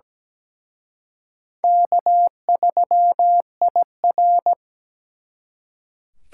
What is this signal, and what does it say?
Morse code, heard as a keyed tone, about 20 wpm, 700 Hz. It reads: K3IR
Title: Speed Racing - USA 1x2 Call Signs 17wpm